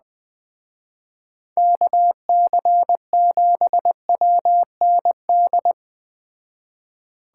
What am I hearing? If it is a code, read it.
Morse code, heard as a keyed tone, about 19 wpm, 700 Hz. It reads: KC7WND
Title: Speed Racing - USA Call Signs 17wpm